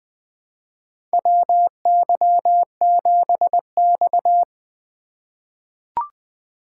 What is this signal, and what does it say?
Morse code, heard as a keyed tone, about 20 wpm, 700 Hz. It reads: WY7X
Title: USA Call Signs - 4 Characters - 2x Character Spacing - 20wpm